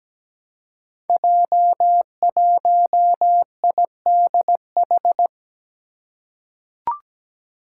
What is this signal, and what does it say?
Morse code, heard as a keyed tone, about 17 wpm, 700 Hz. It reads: J1IDH
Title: Five Challenging Letter-Number Combinations 17wpm